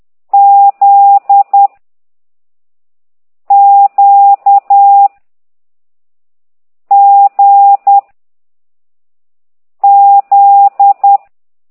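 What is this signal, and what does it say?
Morse code, heard as a keyed tone, about 10 wpm, 800 Hz. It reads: Z Q G Z
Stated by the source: Morse code lesson 5 exercise 10 by CT2IDW, containing only characters M, O, G, Z, Q and comma.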